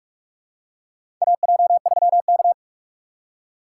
Morse code, heard as a keyed tone, about 44 wpm, 700 Hz. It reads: AJ3K
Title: USA Call Signs - 4 Characters - 6x Character Spacing - 45wpm